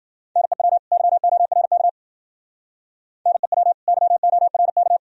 Morse code, heard as a keyed tone, about 50 wpm, 700 Hz. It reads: NEW YORK NEW YORK
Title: Speed-Racing - Sets of 2 Words 40wpm